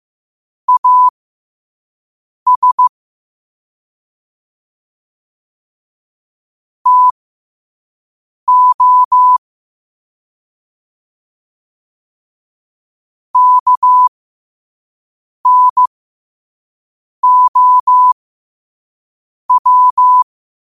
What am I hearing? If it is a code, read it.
Morse code, heard as a keyed tone, about 15 wpm, spaced as for 5 wpm, 1000 Hz. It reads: AS TO KNOW
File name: qotdfiles / 5wpm